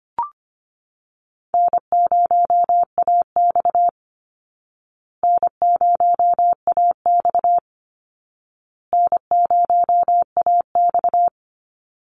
Morse code, heard as a keyed tone, about 25 wpm, 700 Hz. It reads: N0AX N0AX N0AX
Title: Easy Call Signs Repeated 3x - 25wpm